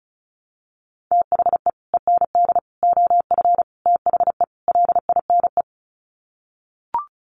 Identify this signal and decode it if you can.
Morse code, heard as a keyed tone, about 34 wpm, 700 Hz. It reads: THE END OF THE LINE
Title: Sets of 5 Words Encore2 35wpm